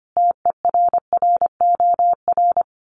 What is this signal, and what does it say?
Morse code, heard as a keyed tone, about 25 wpm, 700 Hz. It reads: TERROR
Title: Sets of 5 Words Encore2 25wpm